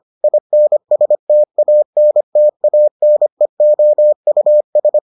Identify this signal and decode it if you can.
Morse code, heard as a keyed tone, about 25 wpm, 600 Hz. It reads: INSTANTANEOUS